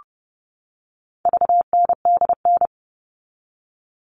Morse code, heard as a keyed tone, about 30 wpm, 700 Hz. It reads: VNDN
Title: All Cut Numbers - 4-Digit 30wpm